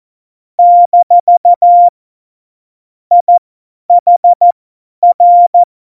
Morse code, heard as a keyed tone, about 14 wpm, spaced as for 10 wpm, 700 Hz. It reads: - IHR
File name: Morsekurs / Witze / Witz 01. Dezember 2020, 50 F70